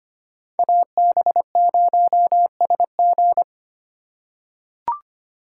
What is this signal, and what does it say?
Morse code, heard as a keyed tone, about 25 wpm, 700 Hz. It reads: AB0SG
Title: USA Call Signs - 5 Characters - 3x Character Spacing - 25wpm